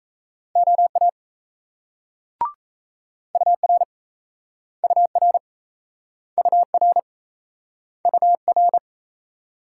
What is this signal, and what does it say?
Morse code, heard as a keyed tone, about 38 wpm, 700 Hz. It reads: OA UR UR UR UR